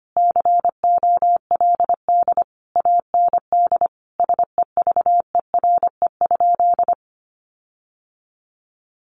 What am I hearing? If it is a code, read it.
Morse code, heard as a keyed tone, about 25 wpm, 700 Hz. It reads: COLD AND SEVERE?
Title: Sentences from Top 1400 Words 25wpm